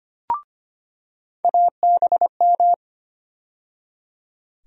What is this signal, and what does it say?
Morse code, heard as a keyed tone, about 25 wpm, 700 Hz. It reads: ABM
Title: USA Call Sign Postfixes - 3 Letters 25wpm